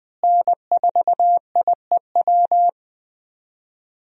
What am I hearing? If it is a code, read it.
Morse code, heard as a keyed tone, about 20 wpm, 700 Hz. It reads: N4IEW
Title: USA Call Signs - 5 Characters - Repeated 2x - 20wpm